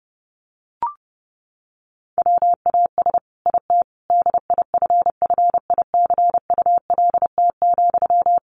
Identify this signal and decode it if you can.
Morse code, heard as a keyed tone, about 30 wpm, 700 Hz. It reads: WAS IT DIFFICULT,
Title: Sentences from Top 1200 Words 30wpm